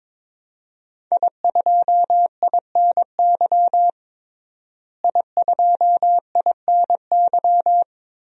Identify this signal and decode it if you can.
Morse code, heard as a keyed tone, about 21 wpm, 700 Hz. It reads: I2INY I2INY
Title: DXCC-100 Call Signs - 5 Characters - Repeated 3x - 22wpm